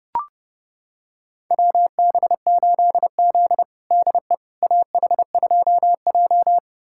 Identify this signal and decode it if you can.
Morse code, heard as a keyed tone, about 30 wpm, 700 Hz. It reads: WB8Z DE AH2J
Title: USA Callsign DE Callsign - 4 Characters - 30wpm